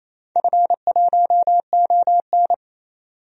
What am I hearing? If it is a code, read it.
Morse code, heard as a keyed tone, about 28 wpm, 700 Hz. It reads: F1ON
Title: DXCC-100 Call Signs - 4 Characters - 28wpm